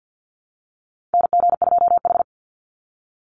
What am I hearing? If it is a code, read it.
Morse code, heard as a keyed tone, about 49 wpm, 700 Hz. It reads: NG2H
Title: Call Signs - 50wpm